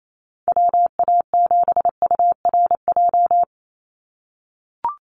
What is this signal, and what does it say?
Morse code, heard as a keyed tone, about 28 wpm, 700 Hz. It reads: WA7URJ
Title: USA Call Signs 28wpm